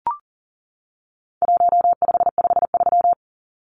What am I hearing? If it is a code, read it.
Morse code, heard as a keyed tone, about 37 wpm, 700 Hz. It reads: 1553
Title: Numbers - 4-Digit 40wpm